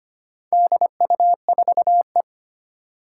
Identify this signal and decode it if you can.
Morse code, heard as a keyed tone, about 25 wpm, 700 Hz. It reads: DU4E
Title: Four Challenging Letter-Number Combinations 25wpm